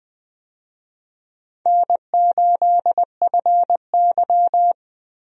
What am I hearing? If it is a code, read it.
Morse code, heard as a keyed tone, about 20 wpm, 700 Hz. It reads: N8FY
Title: USA Call Signs - 4 Characters - Repeated 3x - 20wpm